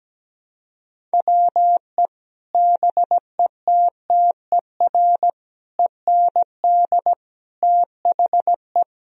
Morse code, heard as a keyed tone, about 17 wpm, 700 Hz. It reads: WE BETTER END THE
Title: Sentences from Top 1000 Words - Review Part 3 - 17wpm